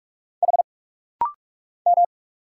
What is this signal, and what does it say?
Morse code, heard as a keyed tone, about 44 wpm, 700 Hz. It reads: H M
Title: Single Letter-Number Alt - Rapid-Fire 45wpm